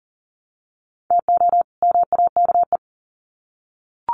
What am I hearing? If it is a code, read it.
Morse code, heard as a keyed tone, about 39 wpm, 700 Hz. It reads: TO MAKE
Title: Sets of 2 Words 40wpm